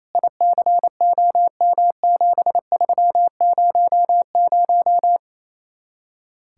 Morse code, heard as a keyed tone, about 35 wpm, 700 Hz. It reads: ICOM7300
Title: Speed-Racing Common QSO Phrases Encore - 28wpm